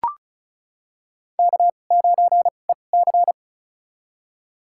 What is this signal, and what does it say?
Morse code, heard as a keyed tone, about 34 wpm, spaced as for 25 wpm, 700 Hz. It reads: K9EC
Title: USA Call Signs - 2x Character Spacing - 35wpm